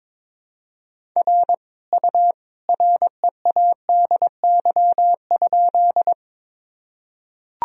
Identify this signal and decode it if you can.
Morse code, heard as a keyed tone, about 22 wpm, 700 Hz. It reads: R U READY?
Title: Sentences from Top 400 Words and Abbreviations 22wpm